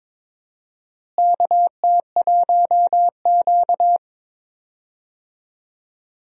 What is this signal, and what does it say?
Morse code, heard as a keyed tone, about 22 wpm, 700 Hz. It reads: KT1Q
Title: USA Call Signs - 4 Characters - Repeated 3x - 22wpm